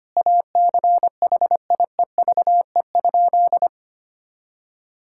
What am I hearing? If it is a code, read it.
Morse code, heard as a keyed tone, about 25 wpm, 700 Hz. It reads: ACHIEVE?
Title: Sentences from Top 1000 Words - Review Part 6 - 25wpm